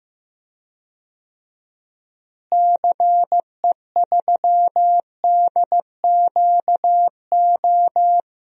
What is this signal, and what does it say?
Morse code, heard as a keyed tone, about 15 wpm, 700 Hz. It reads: CE3DQO
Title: DXCC-100 Call Signs - 6 Characters - Repeated 2x - 15wpm